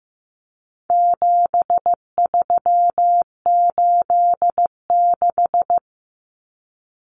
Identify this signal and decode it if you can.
Morse code, heard as a keyed tone, about 15 wpm, 700 Hz. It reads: 7386
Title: Common Cut Numbers - 4-Digit 15wpm